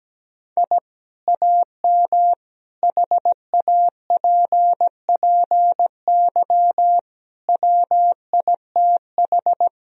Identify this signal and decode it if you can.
Morse code, heard as a keyed tone, about 17 wpm, 700 Hz. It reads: I AM HAPPY WITH
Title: Sentences from Top 1000 Words - Review Part 2 - 17wpm